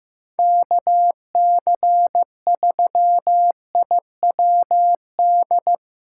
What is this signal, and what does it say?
Morse code, heard as a keyed tone, about 15 wpm, 700 Hz. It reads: KC3IWD
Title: USA Callsign DE Callsign - 6 Characters - 15wpm